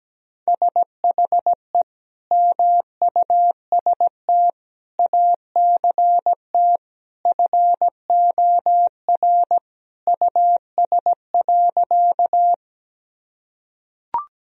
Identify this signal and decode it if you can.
Morse code, heard as a keyed tone, about 17 wpm, 700 Hz. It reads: SHE MUST ACT FOR US.
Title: Sentences from Top 1000 Words - Review Part 1 - 17wpm